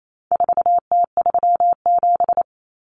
Morse code, heard as a keyed tone, about 28 wpm, 700 Hz. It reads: 4T37
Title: Common Cut Numbers - 4-Digit 28wpm